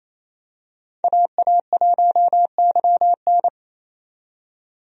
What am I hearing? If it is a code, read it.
Morse code, heard as a keyed tone, about 28 wpm, 700 Hz. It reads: AA1YN
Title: USA Call Signs - 5 Characters - 28wpm